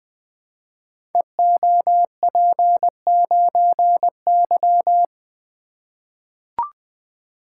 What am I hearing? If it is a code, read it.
Morse code, heard as a keyed tone, about 20 wpm, 700 Hz. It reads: EOP9Y
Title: Code Groups - 5 Letter:Numbers - 20wpm